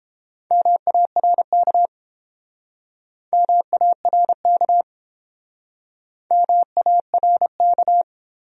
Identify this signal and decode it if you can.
Morse code, heard as a keyed tone, about 30 wpm, 700 Hz. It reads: MARK MARK MARK